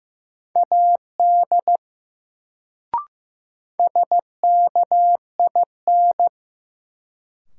Morse code, heard as a keyed tone, about 15 wpm, 700 Hz. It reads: AD SKIN
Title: Sentences from Top 1300 Words - New Word Drill Alt - 15wpm